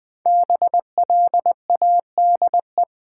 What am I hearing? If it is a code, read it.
Morse code, heard as a keyed tone, about 20 wpm, 700 Hz. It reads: BLADE
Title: 5 Letter Words - First 3 Letters Given - 20wpm